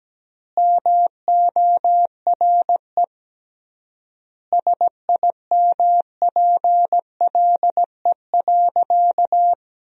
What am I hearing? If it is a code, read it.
Morse code, heard as a keyed tone, about 17 wpm, 700 Hz. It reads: MORE SIMPLE.
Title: 3x Word-Spacing - Sentences from Top 500 Words - 17wpm